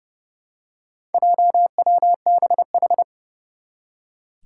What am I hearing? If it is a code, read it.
Morse code, heard as a keyed tone, about 30 wpm, 700 Hz. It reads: JWBH
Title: Four Challenging Letter-Number Combinations 30wpm